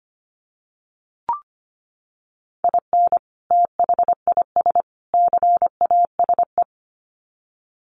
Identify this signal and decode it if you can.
Morse code, heard as a keyed tone, about 25 wpm, 700 Hz. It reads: IN THIS CASE
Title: Sets of 3 Words 25wpm